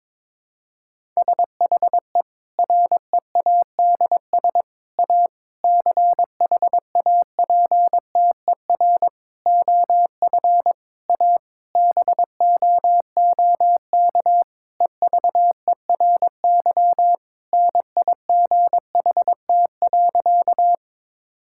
Morse code, heard as a keyed tone, about 22 wpm, 700 Hz. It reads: SHE READS A CHAPTER OF A BOOK EVERY NIGHT.
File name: Difficult Sentences with Hints - 22wpm